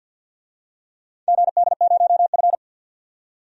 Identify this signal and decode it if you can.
Morse code, heard as a keyed tone, about 51 wpm, 700 Hz. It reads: GD0F
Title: DXCC-100 Call Signs - 4 Characters - 50wpm